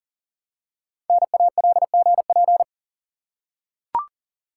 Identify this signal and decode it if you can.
Morse code, heard as a keyed tone, about 41 wpm, 700 Hz. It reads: NALGP
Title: Code Groups - 5 Letters - 40wpm